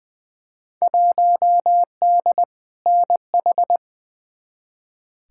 Morse code, heard as a keyed tone, about 21 wpm, 700 Hz. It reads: 1D NH
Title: Field Day Exchanges - 20wpm